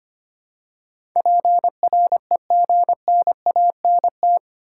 Morse code, heard as a keyed tone, about 25 wpm, 700 Hz. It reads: PREGNANT
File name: Eight Letter Words - 25wpm